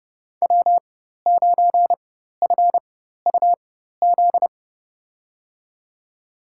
Morse code, heard as a keyed tone, about 32 wpm, spaced as for 14 wpm, 700 Hz. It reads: W9FUZ
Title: USA Call Signs - 5 Characters - 4x Character Spacing - 30wpm